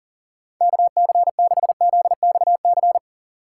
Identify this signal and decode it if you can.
Morse code, heard as a keyed tone, about 51 wpm, 700 Hz. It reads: KC6ZXC